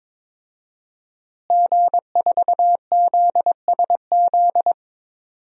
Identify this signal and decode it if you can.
Morse code, heard as a keyed tone, about 22 wpm, 700 Hz. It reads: G4ZSZ
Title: DXCC-100 Call Signs - 5 Characters - Repeated 3x - 22wpm